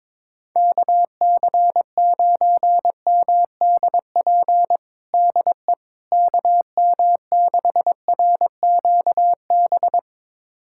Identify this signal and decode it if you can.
Morse code, heard as a keyed tone, about 21 wpm, 700 Hz. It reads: KC9MDP DE KM6RQB
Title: USA Callsign DE Callsign - 6 Characters - 22wpm